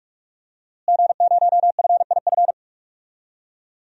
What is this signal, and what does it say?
Morse code, heard as a keyed tone, about 46 wpm, 700 Hz. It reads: G0FIF
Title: DXCC-100 Call Signs - 5 Characters - Repeated 3x - 45wpm